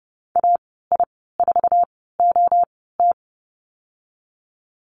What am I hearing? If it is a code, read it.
Morse code, heard as a keyed tone, about 30 wpm, spaced as for 17 wpm, 700 Hz. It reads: AI4OT
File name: Lesson 15g - Number 5 - Call Signs 30wpm